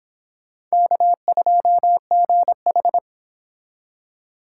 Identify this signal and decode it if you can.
Morse code, heard as a keyed tone, about 26 wpm, 700 Hz. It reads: K2GH